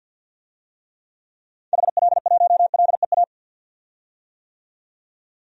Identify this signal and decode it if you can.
Morse code, heard as a keyed tone, about 51 wpm, 700 Hz. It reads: HL1LEA
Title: DXCC-100 Call Signs - 6 Characters - 50wpm